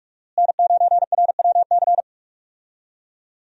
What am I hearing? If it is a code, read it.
Morse code, heard as a keyed tone, about 44 wpm, 700 Hz. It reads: N9RWC
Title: USA Call Signs - 5 Characters - Repeated 3x - 45wpm